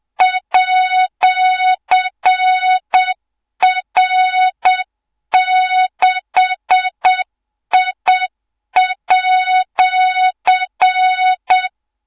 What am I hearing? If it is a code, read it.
Morse code, heard as a keyed tone, about 7 wpm, 3700 Hz, with noise. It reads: @R6I@
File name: Morze-HA2MN / hang / 24 GYAKORLAT